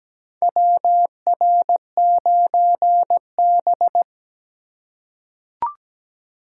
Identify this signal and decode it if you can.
Morse code, heard as a keyed tone, about 17 wpm, 700 Hz. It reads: WR9B
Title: USA Call Signs - 4 Characters - 3x Character Spacing - 17wpm